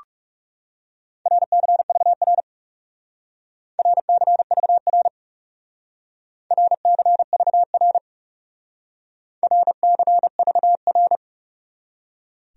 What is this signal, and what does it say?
Morse code, heard as a keyed tone, about 39 wpm, 700 Hz. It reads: RCVR RCVR RCVR RCVR